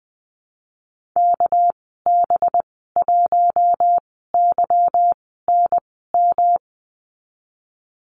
Morse code, heard as a keyed tone, about 20 wpm, spaced as for 14 wpm, 700 Hz. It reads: KB1YNM
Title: 219 Lesson 35g - Number 0 - Call Signs 20wpm